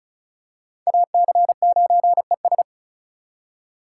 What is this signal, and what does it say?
Morse code, heard as a keyed tone, about 36 wpm, 700 Hz. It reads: AC9ES
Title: USA Call Signs - 3x Character Spacing - 35wpm